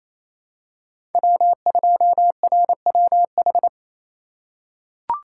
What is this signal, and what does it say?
Morse code, heard as a keyed tone, about 28 wpm, 700 Hz. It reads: W2RWH
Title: USA Call Signs - 5 Characters - 2x Character Spacing - 28wpm